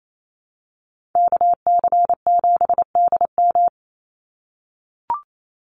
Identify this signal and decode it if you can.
Morse code, heard as a keyed tone, about 28 wpm, 700 Hz. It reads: KC7DM
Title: NA SOTA Activator Callsigns 28wpm